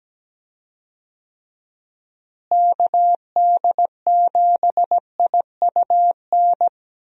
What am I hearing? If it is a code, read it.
Morse code, heard as a keyed tone, about 17 wpm, 700 Hz. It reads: KD7IUN
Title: USA Call Signs - 6 Characters - Repeated 3x - 17wpm